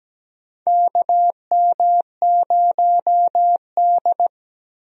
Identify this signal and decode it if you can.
Morse code, heard as a keyed tone, about 17 wpm, 700 Hz. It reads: KM0D
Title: USA Call Signs - 4 Characters - Repeated 3x - 17wpm